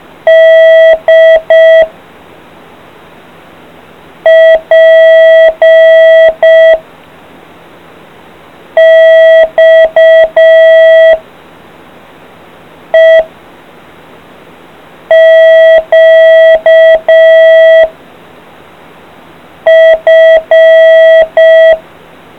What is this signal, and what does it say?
Morse code, heard as a keyed tone, about 5 wpm, 650 Hz. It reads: D P X E Q F